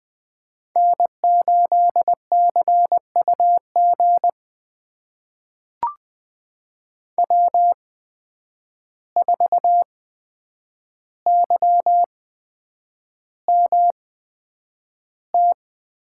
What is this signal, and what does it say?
Morse code, heard as a keyed tone, about 20 wpm, spaced as for 9 wpm, 700 Hz. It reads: N8CUG W4YMT
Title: USA Call Signs - 5 Characters - 8x Character Spacing - 20wpm